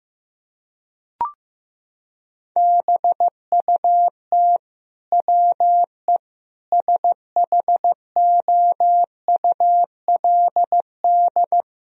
Morse code, heard as a keyed tone, about 16 wpm, 700 Hz. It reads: BUT WE SHOULD
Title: Sentences from Top 1000 Words - Review Part 4 - 15wpm